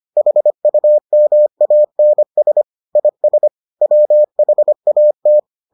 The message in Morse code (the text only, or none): HUMANS IS WHAT